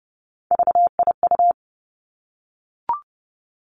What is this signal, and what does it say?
Morse code, heard as a keyed tone, about 30 wpm, 700 Hz. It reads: VIU
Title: Three Challenging Letter-Number Combinations 30wpm